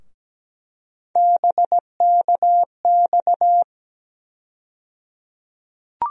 Morse code, heard as a keyed tone, about 17 wpm, 700 Hz. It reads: BKX
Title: USA Call Sign Postfixes - 3 Letters 17wpm